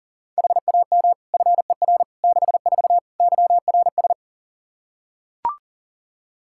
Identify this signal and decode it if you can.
Morse code, heard as a keyed tone, about 37 wpm, 690 Hz. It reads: HAM FER 64 YRS
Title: Practice QSO - Exchange 3 - More Info - 40wpm